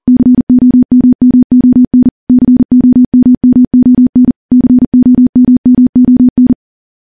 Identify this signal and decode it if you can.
Morse code, heard as a keyed tone, about 42 wpm, 250 Hz. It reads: COMMON COMMON COMMON
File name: morse / morse-training-40WPM